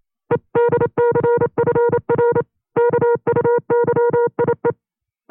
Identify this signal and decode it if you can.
Morse code, heard as a keyed tone, about 28 wpm, 480 Hz. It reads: EDCFR KUYIE
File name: 25WPM